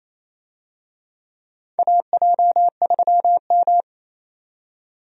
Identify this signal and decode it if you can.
Morse code, heard as a keyed tone, about 28 wpm, 700 Hz. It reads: AJ3M
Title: USA Call Signs - 4 Characters - 28wpm